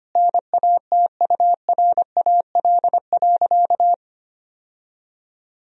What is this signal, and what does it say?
Morse code, heard as a keyed tone, about 25 wpm, 700 Hz. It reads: NATURAL.
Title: Sentences from Top 1000 Words - Review Part 4 - 25wpm